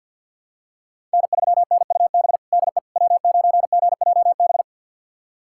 Morse code, heard as a keyed tone, about 47 wpm, 700 Hz. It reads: N3NUB DE W9GJB
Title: USA Callsign DE Callsign - 5 Characters - 50wpm